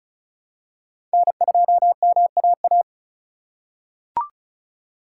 Morse code, heard as a keyed tone, about 34 wpm, 700 Hz. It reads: N2MAA
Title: USA Call Signs - 5 Characters - 3x Character Spacing - 35wpm